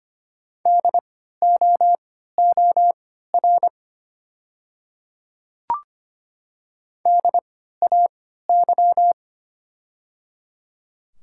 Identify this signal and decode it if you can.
Morse code, heard as a keyed tone, about 25 wpm, spaced as for 14 wpm, 700 Hz. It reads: DOOR DAY
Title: 3x Character Spacing - Top 300 Words - 25wpm